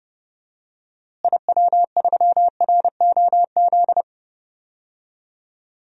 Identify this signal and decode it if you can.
Morse code, heard as a keyed tone, about 31 wpm, 700 Hz. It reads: IW3ROZ
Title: DXCC-100 Call Signs - 6 Characters - Repeated 2x - 30wpm